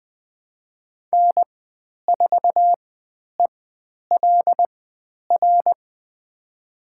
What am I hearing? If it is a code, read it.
Morse code, heard as a keyed tone, about 20 wpm, spaced as for 10 wpm, 700 Hz. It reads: N4ELR
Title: 095 Lesson 16g - Letter c - Call Signs 20wpm